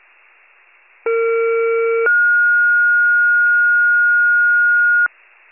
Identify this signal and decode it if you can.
Two-tone page, heard as a one-second tone, then a three-second tone, about 460 then 1430 Hz. It is heard from 1.1 to 5.1 s.